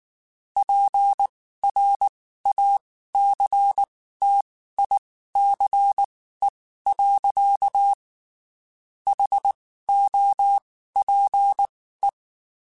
Morse code, heard as a keyed tone, about 19 wpm, spaced as for 14 wpm, 800 Hz. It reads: PRACTICE. HOPE
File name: (spaced) / QSOs Spaced-mp3s / WD0FBT @18 sp